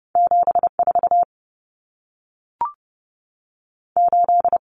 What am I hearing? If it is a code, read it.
Morse code, heard as a keyed tone, about 30 wpm, 700 Hz. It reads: 7 4 8
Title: Numbers - 2-Digit 30wpm